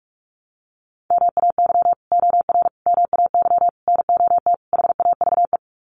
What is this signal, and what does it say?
Morse code, heard as a keyed tone, about 44 wpm, 700 Hz. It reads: MAY OR MAY NOT HAVE
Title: Sets of 5 Words Encore2 45wpm